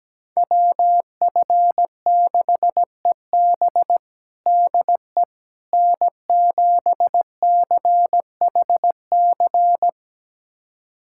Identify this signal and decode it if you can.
Morse code, heard as a keyed tone, about 17 wpm, 700 Hz. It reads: WF6EB DE N7CHC
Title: USA Callsign DE Callsign - 5 Characters - 17wpm